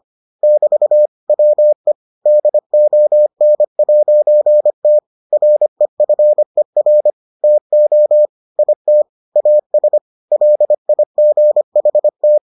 Morse code, heard as a keyed tone, about 25 wpm, 600 Hz. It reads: <BT> WE DON'T REFER TO IT AS LIGHT